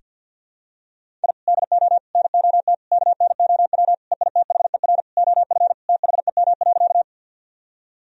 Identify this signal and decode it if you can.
Morse code, heard as a keyed tone, about 52 wpm, 700 Hz. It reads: I DO NOT KNOW EITHER OF THEM.
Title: Sentences from Top 1000 Words - Review Part 4 - 50wpm